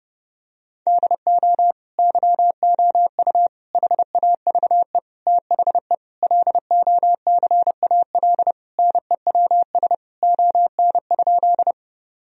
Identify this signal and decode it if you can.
Morse code, heard as a keyed tone, about 30 wpm, 700 Hz. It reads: DO YOU HAVE THE LOCAL NEWS ON?
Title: Sentences from Top 1000 Words - Review Part 3 - 30wpm